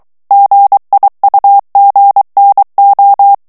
Morse code, heard as a keyed tone, about 23 wpm, 800 Hz. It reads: GIUGNO
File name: ITALIEN texte clair 1 vitesse 22 mots minute